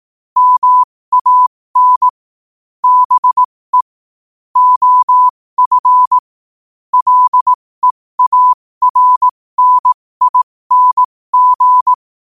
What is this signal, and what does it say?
Morse code, heard as a keyed tone, about 18 wpm, spaced as for 15 wpm, 1000 Hz. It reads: MAN BE OF LEARNING
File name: qotdfiles / 15wpm